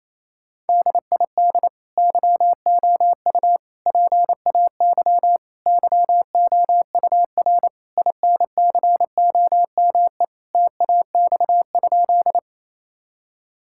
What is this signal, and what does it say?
Morse code, heard as a keyed tone, about 28 wpm, 700 Hz. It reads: DID YOU PAY YOUR INCOME TAX?
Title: Sentences from Top 1000 Words - Review Part 5 - 28wpm